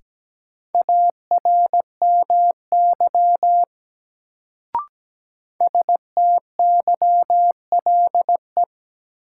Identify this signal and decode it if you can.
Morse code, heard as a keyed tone, about 17 wpm, 700 Hz. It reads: ARMY STYLE
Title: Sentences from Top 800 Words - New Word Drill - 17wpm